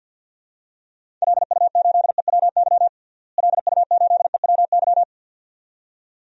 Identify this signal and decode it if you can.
Morse code, heard as a keyed tone, about 52 wpm, 700 Hz. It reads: LU8EWY LU8EWY
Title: DXCC-100 Call Signs - 6 Characters - Repeated 2x - 50wpm